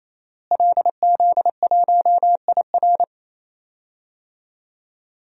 Morse code, heard as a keyed tone, about 28 wpm, 700 Hz. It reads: LZ1IR
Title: DXCC-100 Call Signs - 5 Characters - Repeated 3x - 28wpm